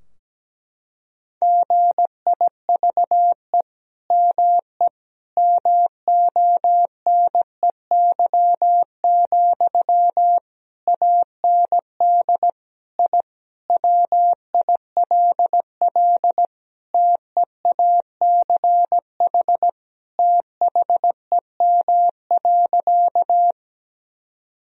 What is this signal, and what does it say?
Morse code, heard as a keyed tone, about 17 wpm, 700 Hz. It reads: GIVE ME MONEY, AND I WILL TEACH THEM.
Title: Sentences from Top 1000 Words - Review Part 2 - 17wpm